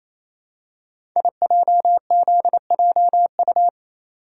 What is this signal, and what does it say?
Morse code, heard as a keyed tone, about 27 wpm, 700 Hz. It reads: IJZJU
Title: Code Groups - 5 Letter:Numbers - 28wpm